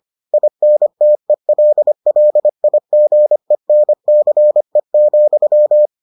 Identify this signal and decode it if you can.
Morse code, heard as a keyed tone, about 25 wpm, 600 Hz. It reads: INTELLIGENCE,